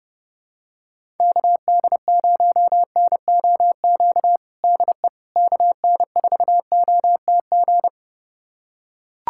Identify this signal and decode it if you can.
Morse code, heard as a keyed tone, about 30 wpm, 700 Hz. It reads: KD0NOQ DE KN4OTG
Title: USA Callsign DE Callsign - 6 Characters - 30wpm